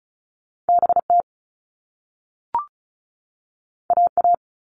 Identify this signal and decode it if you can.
Morse code, heard as a keyed tone, about 36 wpm, 690 Hz. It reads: BT AA
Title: All Cut Numbers - 2-Digit 35wpm